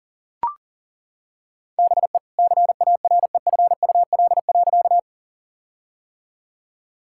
Morse code, heard as a keyed tone, about 40 wpm, 700 Hz. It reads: BE CAREFUL.
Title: Sentences from Top 1600 Words 40wpm